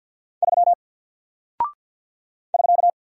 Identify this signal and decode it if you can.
Morse code, heard as a keyed tone, about 51 wpm, 700 Hz. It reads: 3 <SK>
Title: Single Character (Letter:Number:Punctuation:Prosign) Alt 50wpm